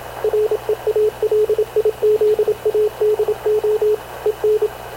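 Morse code, heard as a keyed tone, about 27 wpm, 430 Hz, with radio noise. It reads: REALIZADO EN
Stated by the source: Havana Radio / CLA sending Spanish press (PX) in Morse, recorded by Radio Officer Paul Dumesnil onboard CNAV QUEST/CZDO.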